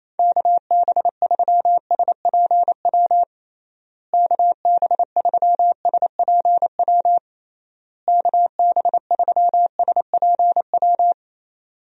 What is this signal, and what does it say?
Morse code, heard as a keyed tone, about 28 wpm, 700 Hz. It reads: KB3SPW KB3SPW KB3SPW
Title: USA Call Signs - 6 Characters - Repeated 3x - 28wpm